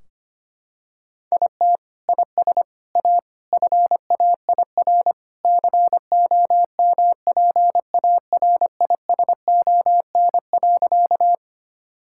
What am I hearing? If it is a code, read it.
Morse code, heard as a keyed tone, about 25 wpm, 700 Hz. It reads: IT IS A FAIR COMPARISON.
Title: Sentences from Top 1600 Words 25wpm